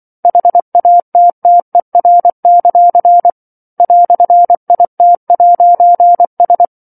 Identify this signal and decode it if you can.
Morse code, heard as a keyed tone, about 24 wpm, 700 Hz. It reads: HATTER; "IT'S
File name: 07-Chapter-07 24 WPM